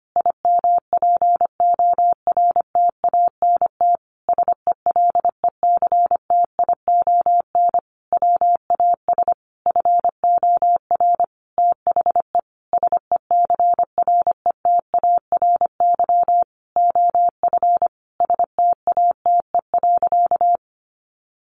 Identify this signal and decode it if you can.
Morse code, heard as a keyed tone, about 25 wpm, 700 Hz. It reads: IMPORTANT SELECTION WAS FOR THE SECRETARY OF STATE.
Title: Sentences from Top 1400 Words 25wpm